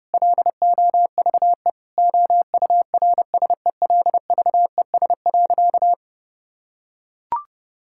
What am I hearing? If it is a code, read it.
Morse code, heard as a keyed tone, about 30 wpm, 700 Hz. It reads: LOVE OURSELVES.
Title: Sentences from Top 1600 Words 30wpm